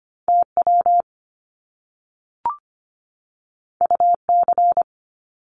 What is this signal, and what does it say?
Morse code, heard as a keyed tone, about 25 wpm, 700 Hz. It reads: TW UC
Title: Top Two Letter Combinations in Words 25wpm